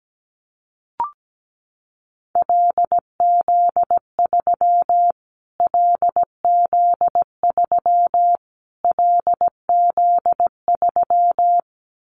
Morse code, heard as a keyed tone, about 17 wpm, 700 Hz. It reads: LZ3 LZ3 LZ3
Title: Callsign Prefixes for Top 25 DXCC - Repeated 3x - 17wpm